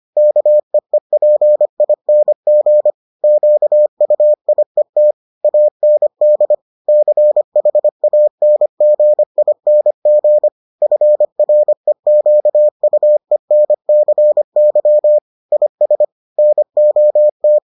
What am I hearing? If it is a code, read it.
Morse code, heard as a keyed tone, about 25 wpm, 600 Hz. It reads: KEEPING QUIET AND CHANGING FREQUENCY IS NOT